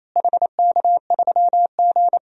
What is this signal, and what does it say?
Morse code, heard as a keyed tone, about 27 wpm, 700 Hz. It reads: HK3G
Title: DXCC-100 Call Signs - 4 Characters - Repeated 3x - 28wpm